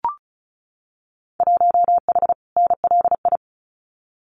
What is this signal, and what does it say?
Morse code, heard as a keyed tone, about 34 wpm, 700 Hz. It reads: 1H NLI
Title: Winter Field Day Exchanges 35wpm